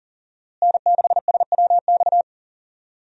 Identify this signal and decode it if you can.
Morse code, heard as a keyed tone, about 40 wpm, 700 Hz. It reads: N6SWX
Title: USA Call Signs - 5 Characters - Repeated 2x - 40wpm